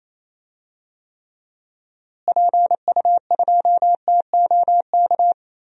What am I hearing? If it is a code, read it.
Morse code, heard as a keyed tone, about 27 wpm, 700 Hz. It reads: PU2TOK
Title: DXCC-100 Call Signs - 6 Characters - Repeated 3x - 28wpm